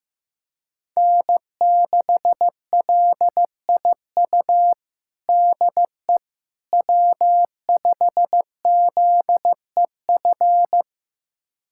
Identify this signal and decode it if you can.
Morse code, heard as a keyed tone, about 15 wpm, 700 Hz. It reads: N6LIU DE W5ZEF
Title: USA Callsign DE Callsign - 5 Characters - 15wpm